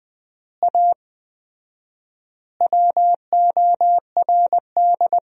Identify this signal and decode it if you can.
Morse code, heard as a keyed tone, about 20 wpm, 700 Hz. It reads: A WORD
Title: 4x Word-Spacing - Sentences from Top 300 Words - 20wpm